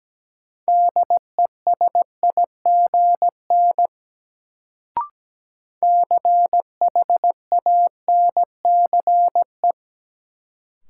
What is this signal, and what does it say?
Morse code, heard as a keyed tone, about 17 wpm, 700 Hz. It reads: DESIGN CHANCE
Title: Sentences from Top 800 Words - New Word Drill - 17wpm